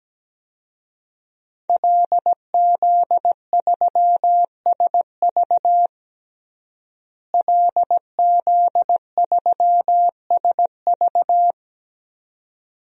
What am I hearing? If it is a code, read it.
Morse code, heard as a keyed tone, about 17 wpm, 700 Hz. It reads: LZ3SV LZ3SV
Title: DXCC-100 Call Signs - 5 Characters - Repeated 2x - 17wpm